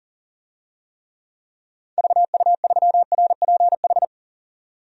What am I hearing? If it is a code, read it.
Morse code, heard as a keyed tone, about 40 wpm, 700 Hz. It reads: VU3RPH
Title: DXCC-100 Call Signs - 6 Characters - 40wpm